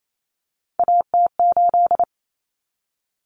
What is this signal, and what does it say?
Morse code, heard as a keyed tone, about 28 wpm, 700 Hz. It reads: AT8
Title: Common Cut Numbers - 3-Digit 28wpm